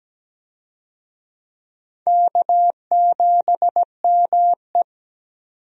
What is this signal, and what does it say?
Morse code, heard as a keyed tone, about 17 wpm, 700 Hz. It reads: K7ME
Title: USA Call Signs - 4 Characters - 17wpm